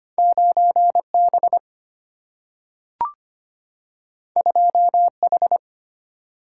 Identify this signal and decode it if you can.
Morse code, heard as a keyed tone, about 25 wpm, 700 Hz. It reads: 9B 2H
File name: Two Letter-Number Combinations - 25wpm